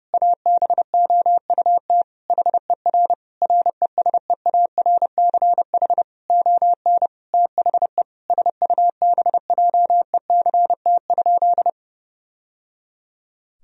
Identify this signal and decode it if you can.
Morse code, heard as a keyed tone, about 30 wpm, 700 Hz. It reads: ABOUT HER RESEARCH ON THE SUBJECT?
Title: Sentences from Top 1000 Words - Review Part 4 - 30wpm